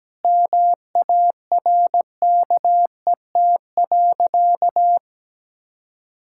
Morse code, heard as a keyed tone, about 17 wpm, 700 Hz. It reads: MARKET.
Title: Sentences from Top 1600 Words 17wpm